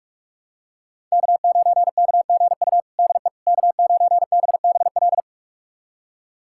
Morse code, heard as a keyed tone, about 47 wpm, 700 Hz. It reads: K9KGU DE K9BBL
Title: USA Callsign DE Callsign - 5 Characters - 45wpm